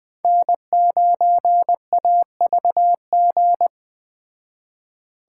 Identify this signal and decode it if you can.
Morse code, heard as a keyed tone, about 20 wpm, 700 Hz. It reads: N9AVG
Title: Speed Racing - USA 1x3 Call Signs 17wpm